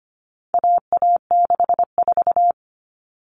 Morse code, heard as a keyed tone, about 25 wpm, 700 Hz. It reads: AA64
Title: All Cut Numbers - 4-Digit 25wpm